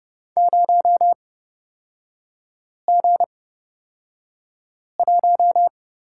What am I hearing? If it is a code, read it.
Morse code, heard as a keyed tone, about 30 wpm, spaced as for 5 wpm, 700 Hz. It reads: 0G1
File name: Lesson 37e - Letter x - Three Characters 30wpm